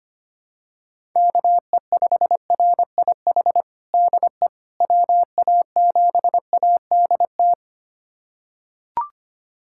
Morse code, heard as a keyed tone, about 25 wpm, 700 Hz. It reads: KE5RIH DE WA7ADT
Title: USA Callsign DE Callsign - 6 Characters - 25wpm